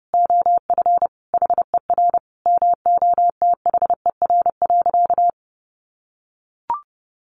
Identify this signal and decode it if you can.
Morse code, heard as a keyed tone, about 28 wpm, 700 Hz. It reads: OF HER MOTHER.
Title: Sentences from Top 800 Words 30wpm